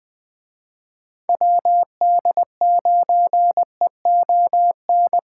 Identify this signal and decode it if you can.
Morse code, heard as a keyed tone, about 20 wpm, 700 Hz. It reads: WD9EON
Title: Speed Racing - USA Call Signs 17wpm